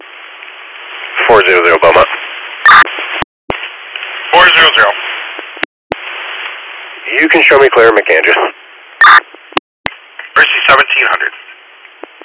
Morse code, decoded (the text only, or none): none